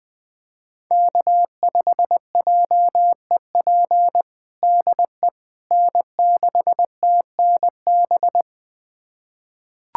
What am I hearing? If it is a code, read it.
Morse code, heard as a keyed tone, about 20 wpm, 700 Hz. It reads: K5JEP DE N6TNB
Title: USA Callsign DE Callsign - 5 Characters - 20wpm